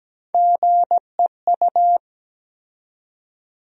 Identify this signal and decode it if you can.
Morse code, heard as a keyed tone, about 17 wpm, 700 Hz. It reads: GEU
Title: USA Call Sign Postfixes - 3 Letters 17wpm